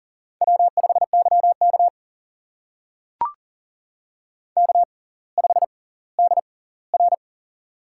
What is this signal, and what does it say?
Morse code, heard as a keyed tone, about 40 wpm, spaced as for 20 wpm, 700 Hz. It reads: W5YK K5DR
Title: USA Call Signs - 6x Character Spacing - 40wpm